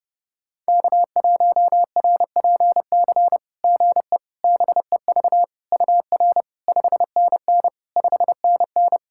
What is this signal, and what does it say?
Morse code, heard as a keyed tone, about 30 wpm, 700 Hz. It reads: K1RPC GE BEV UR 5NN 5NN
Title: SOTA QSO Practice 30wpm